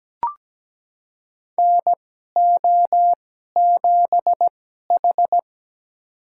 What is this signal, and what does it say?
Morse code, heard as a keyed tone, about 17 wpm, spaced as for 12 wpm, 700 Hz. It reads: NO7H
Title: USA Call Signs - 2x Character Spacing - 17wpm